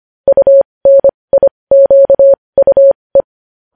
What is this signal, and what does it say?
Morse code, heard as a keyed tone, about 25 wpm, spaced as for 20 wpm, 550 Hz. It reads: UNIQUE
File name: podcast / News headlines-1x-25x20